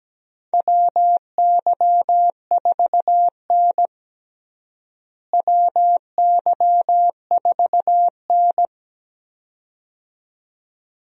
Morse code, heard as a keyed tone, about 17 wpm, 700 Hz. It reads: WY4N WY4N
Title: USA Call Signs - 4 Characters - Repeated 2x - 17wpm